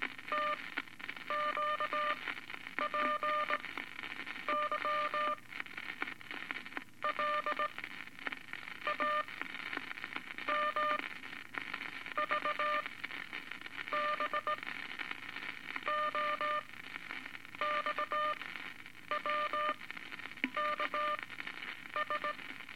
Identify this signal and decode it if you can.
Morse code, heard as a keyed tone, about 19 wpm, spaced as for 7 wpm, 1200 Hz, with noise. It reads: TQPYL AMVBO XWKS
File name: Eight Words Per Minute